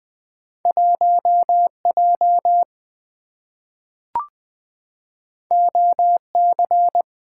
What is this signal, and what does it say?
Morse code, heard as a keyed tone, about 20 wpm, 700 Hz. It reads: 1J OC
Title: Two Letter-Number Combinations - 20wpm